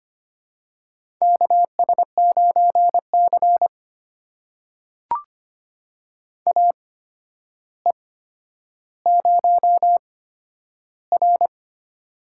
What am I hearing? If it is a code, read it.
Morse code, heard as a keyed tone, about 25 wpm, spaced as for 11 wpm, 700 Hz. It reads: KS9C AE0R
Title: USA Call Signs - 4 Characters - 8x Character Spacing - 25wpm